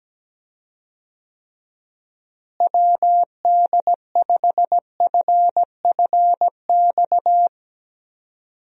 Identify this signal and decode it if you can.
Morse code, heard as a keyed tone, about 17 wpm, 700 Hz. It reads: WD5FFX
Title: USA Call Signs - 6 Characters - Repeated 3x - 17wpm